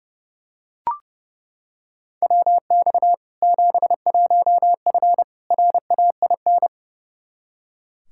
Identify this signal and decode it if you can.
Morse code, heard as a keyed tone, about 30 wpm, 700 Hz. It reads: WX 71F RAIN
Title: K1USN SST QSO Practice 30wpm